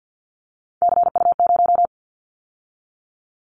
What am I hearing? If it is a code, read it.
Morse code, heard as a keyed tone, about 50 wpm, 700 Hz. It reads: CU0
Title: Callsign Prefixes for Top 100 DXCC - 50wpm